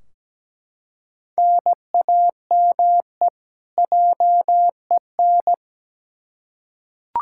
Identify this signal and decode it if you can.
Morse code, heard as a keyed tone, about 17 wpm, 700 Hz. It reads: NAME JEN
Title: K1USN SST QSO Practice 17wpm